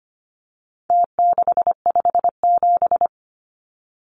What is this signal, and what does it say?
Morse code, heard as a keyed tone, about 25 wpm, 700 Hz. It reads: T657
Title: Numbers - Contest Style - 4-Digit 25wpm